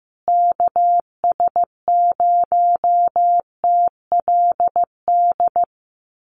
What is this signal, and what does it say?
Morse code, heard as a keyed tone, about 15 wpm, 700 Hz. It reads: KS0TLD
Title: USA Call Signs Repeated 3x 15wpm